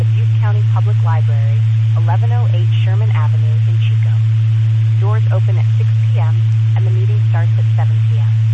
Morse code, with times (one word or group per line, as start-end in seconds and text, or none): none